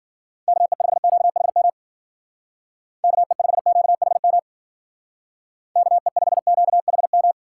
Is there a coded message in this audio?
Yes: Morse code from 0.5 to 7.3 s, heard as a keyed tone, about 51 wpm, 700 Hz. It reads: KE5QHM KE5QHM KE5QHM